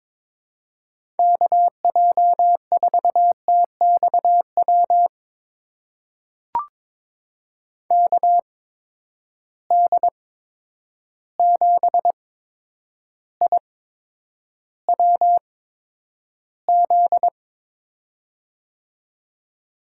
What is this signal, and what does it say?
Morse code, heard as a keyed tone, about 22 wpm, spaced as for 9 wpm, 700 Hz. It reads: KJ4TXW KD7IWZ
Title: USA Call Signs - 6 Characters - 8x Character Spacing - 22wpm